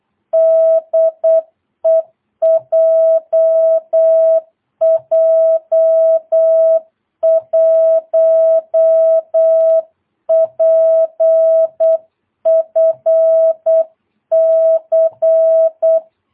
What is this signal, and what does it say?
Morse code, heard as a keyed tone, about 8 wpm, 660 Hz. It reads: DE JJ1PFC
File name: jh8cbh / 2022cw